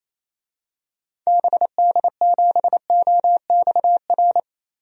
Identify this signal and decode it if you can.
Morse code, heard as a keyed tone, about 28 wpm, 700 Hz. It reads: BD7OXR
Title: DXCC-100 Call Signs - 6 Characters - Repeated 2x - 28wpm